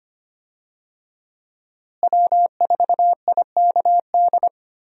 Morse code, heard as a keyed tone, about 25 wpm, 700 Hz. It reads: W4IKD
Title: USA Call Signs - 5 Characters - Repeated 2x - 25wpm